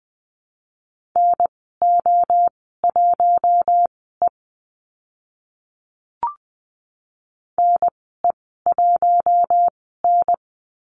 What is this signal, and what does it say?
Morse code, heard as a keyed tone, about 20 wpm, spaced as for 15 wpm, 700 Hz. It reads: NO1E NE1N
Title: 042 Lesson 8g - Number 1 - Call Signs 20wpm